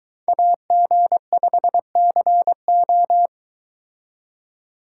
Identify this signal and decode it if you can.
Morse code, heard as a keyed tone, about 23 wpm, 700 Hz. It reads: AG5CO